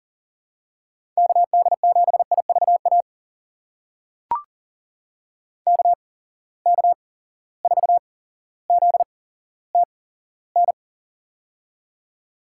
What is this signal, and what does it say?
Morse code, heard as a keyed tone, about 40 wpm, spaced as for 17 wpm, 700 Hz. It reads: KD7IVA KK4ZTN
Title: USA Call Signs - 6 Characters - 8x Character Spacing - 40wpm